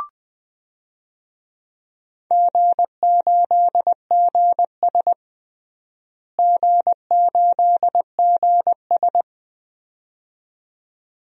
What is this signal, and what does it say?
Morse code, heard as a keyed tone, about 21 wpm, 700 Hz. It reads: G8GS G8GS
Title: DXCC-100 Call Signs - 4 Characters - Repeated 2x - 20wpm